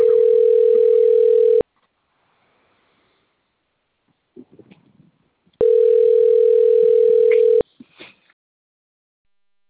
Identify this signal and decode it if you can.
Ringback tone (North America), 440 + 480 Hz.
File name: rg-1600-unknown- - - .95977